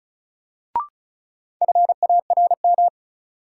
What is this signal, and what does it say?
Morse code, heard as a keyed tone, about 36 wpm, 700 Hz. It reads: FARM
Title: Sentences from Top 1000 Words - Review Part 2 - New Word Drill - 35wpm